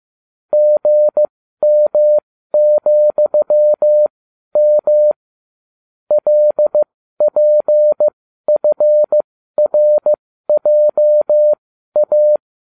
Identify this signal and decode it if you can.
Morse code, heard as a keyed tone, about 15 wpm, spaced as for 12 wpm, 600 Hz. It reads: GM,M LPFRJA